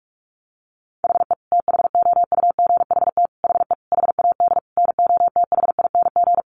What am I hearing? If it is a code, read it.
Morse code, heard as a keyed tone, about 44 wpm, 700 Hz. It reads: HE THOUGHT HE HAD NOTHING
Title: Sentences from Top 600 Words 45wpm